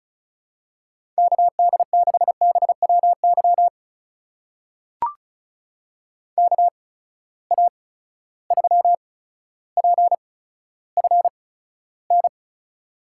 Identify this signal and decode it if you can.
Morse code, heard as a keyed tone, about 36 wpm, spaced as for 15 wpm, 700 Hz. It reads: KD6BWY KA3PFN
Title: USA Call Signs - 6 Characters - 8x Character Spacing - 35wpm